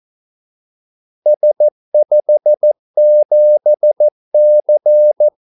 Morse code, heard as a keyed tone, about 14 wpm, 600 Hz. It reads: S57C
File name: Callsigns-14wpm